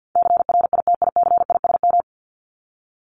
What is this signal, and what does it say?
Morse code, heard as a keyed tone, about 50 wpm, 700 Hz. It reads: CRITICISM
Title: Top 1001-1250 Words 50wpm